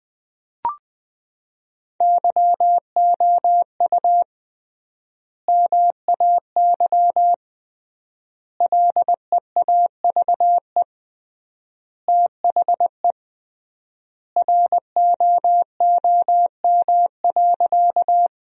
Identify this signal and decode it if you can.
Morse code, heard as a keyed tone, about 20 wpm, 700 Hz. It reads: YOU MAY LEAVE THE ROOM.
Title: 3x Word-Spacing - Sentences from Top 400 Words - 20wpm